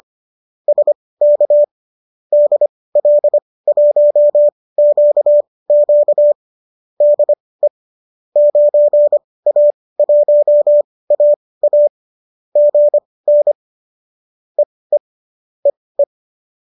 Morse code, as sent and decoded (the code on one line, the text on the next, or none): ... -.- / -.. .-.. .---- --.- --.- / -.. / ----. .- .---- .- .- / --. -. / . / . .
SK DL1QQ DE 9A1AA GN EE EE